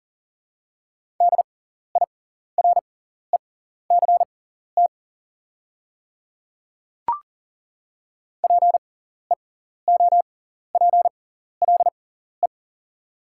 Morse code, heard as a keyed tone, about 41 wpm, spaced as for 14 wpm, 690 Hz. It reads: DIRECT PEOPLE
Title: 6x Character Spacing - Top 500 Words - 40wpm